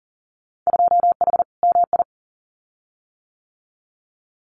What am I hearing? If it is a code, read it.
Morse code, heard as a keyed tone, about 42 wpm, 700 Hz. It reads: 2H MI
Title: Winter Field Day Exchanges Encore 40wpm